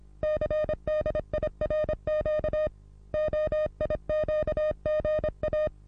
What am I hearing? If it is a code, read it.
Morse code, heard as a keyed tone, about 29 wpm, 610 Hz. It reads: CDIRQ OIQGA
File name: repos / Morsekurs / DJ4UF Morsen27